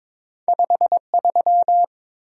Morse code, heard as a keyed tone, about 22 wpm, 700 Hz. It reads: 53
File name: practice / 22wpm / Two Letter-Number Combinations - 22wpm